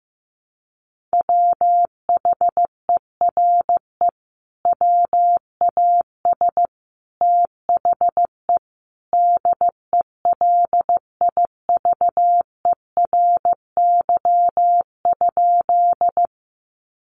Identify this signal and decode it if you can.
Morse code, heard as a keyed tone, about 15 wpm, 700 Hz. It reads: WHERE WAS THE DELIVERY?
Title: Sentences from Top 1500 Words 15wpm